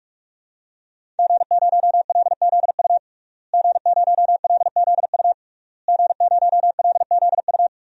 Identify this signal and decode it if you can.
Morse code, heard as a keyed tone, about 43 wpm, 700 Hz. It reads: G0LZU G0LZU G0LZU
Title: DXCC-100 Call Signs - 5 Characters - Repeated 3x - 45wpm